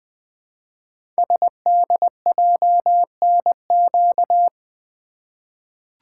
Morse code, heard as a keyed tone, about 20 wpm, 700 Hz. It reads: SDJNQ
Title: Code Groups - 5 Letter:Numbers - 20wpm